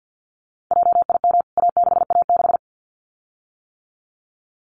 Morse code, heard as a keyed tone, about 50 wpm, 700 Hz. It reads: JIM A6A6
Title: CWT Exchange 50wpm